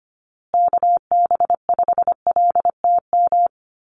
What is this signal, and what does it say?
Morse code, heard as a keyed tone, about 25 wpm, 700 Hz. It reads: KB5LTM
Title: USA Call Signs Repeated 2x 25wpm